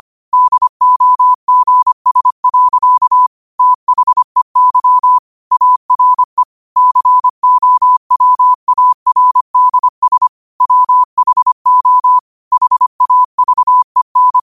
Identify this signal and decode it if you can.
Morse code, heard as a keyed tone, about 25 wpm, 1000 Hz. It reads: DOGS. THEY ARE COWARDS WHO HAVEN
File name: qotdfiles / 25wpm